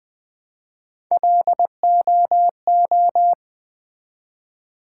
Morse code, heard as a keyed tone, about 20 wpm, 700 Hz. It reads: LOO
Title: Build A Word - 1 Letter At A Time - 4-Letter Words - 20wpm